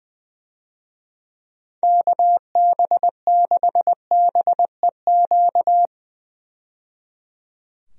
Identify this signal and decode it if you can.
Morse code, heard as a keyed tone, about 20 wpm, 700 Hz. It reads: KB6BEQ
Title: USA Call Signs - 6 Characters - 20wpm